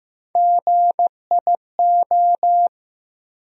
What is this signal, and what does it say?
Morse code, heard as a keyed tone, about 15 wpm, 700 Hz. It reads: GIO
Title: USA Call Sign Postfixes - 3 Letters 15wpm